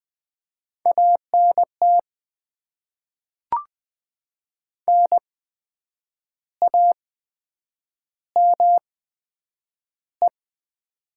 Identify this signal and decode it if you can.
Morse code, heard as a keyed tone, about 20 wpm, spaced as for 8 wpm, 700 Hz. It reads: ANT NAME
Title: Top 20 QSO Elements - 8x Character Spacing - 20wpm